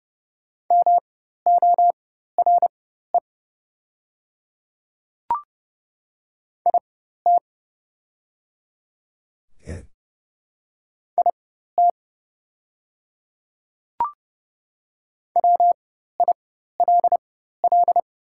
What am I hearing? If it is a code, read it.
Morse code, heard as a keyed tone, about 30 wpm, spaced as for 14 wpm, 700 Hz. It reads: MORE IT IT WILL
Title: 4x Character Spacing - Top 100 Words - 30wpm